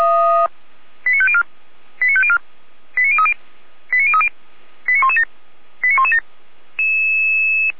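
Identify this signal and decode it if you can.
Five-tone selective call (ZVEI), heard as quick runs of five tones, tones of about 70 ms. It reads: 89504 89504 89030 89030 89117 89117
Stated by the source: ZVEI five-tone alert received by SDR.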